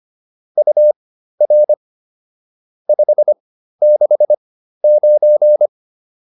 Morse code, heard as a keyed tone, about 25 wpm, spaced as for 13 wpm, 600 Hz. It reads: UR 569